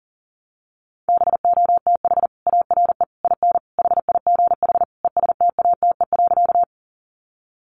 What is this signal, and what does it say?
Morse code, heard as a keyed tone, about 40 wpm, 700 Hz. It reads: BOTH ARE IN HIGH ESTATE.
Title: Sentences from Top 1200 Words 40wpm